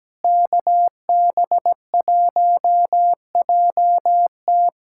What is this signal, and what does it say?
Morse code, heard as a keyed tone, about 17 wpm, 700 Hz. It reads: KB1JT
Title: USA Callsign DE Callsign - 5 Characters - 17wpm